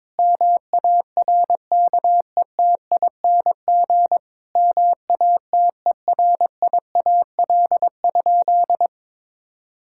Morse code, heard as a keyed tone, about 22 wpm, 700 Hz. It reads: MARKETING MATERIAL?
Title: Sentences from Top 1000 Words - Review Part 5 - 22wpm